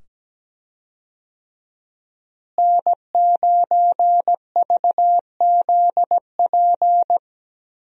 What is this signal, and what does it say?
Morse code, heard as a keyed tone, about 17 wpm, 700 Hz. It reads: N9VZP
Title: USA Call Signs - 5 Characters - Repeated 2x - 17wpm